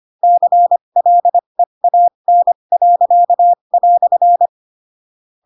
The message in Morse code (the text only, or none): CLEAN."